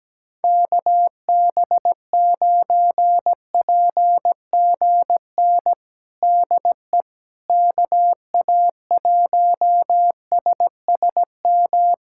Morse code, heard as a keyed tone, about 17 wpm, 700 Hz. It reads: KB9PGN DE KA1SSM
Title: USA Callsign DE Callsign - 6 Characters - 17wpm